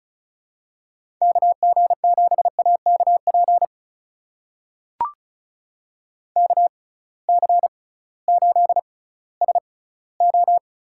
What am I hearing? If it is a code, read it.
Morse code, heard as a keyed tone, about 34 wpm, spaced as for 19 wpm, 700 Hz. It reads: KG7AKP KC8SO
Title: USA Call Signs - 6 Characters - 6x Character Spacing - 35wpm